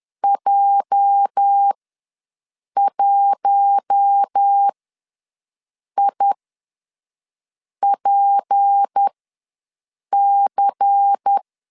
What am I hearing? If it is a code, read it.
Morse code, heard as a keyed tone, about 10 wpm, spaced as for 5 wpm, 800 Hz. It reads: J1IPC
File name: LEZIONI / 8 YJ45 / 8 YJ ws 10wpm 4043